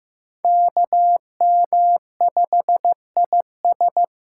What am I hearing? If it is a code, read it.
Morse code, heard as a keyed tone, about 18 wpm, 700 Hz. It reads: KM5IS
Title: Speed Racing - USA 2x2 Call Signs 15wpm